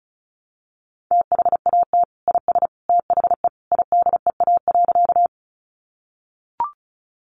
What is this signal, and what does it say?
Morse code, heard as a keyed tone, about 34 wpm, 700 Hz. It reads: THAT IS THE IDEA.
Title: Sentences from Top 300 Words 35wpm